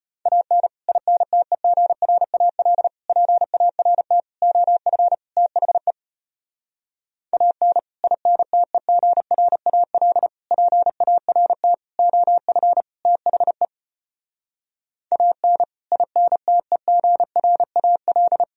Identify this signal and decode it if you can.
Morse code, heard as a keyed tone, about 33 wpm, 700 Hz. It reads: AN INTEGRAL PART OF THE AN INTEGRAL PART OF THE AN INTEGRAL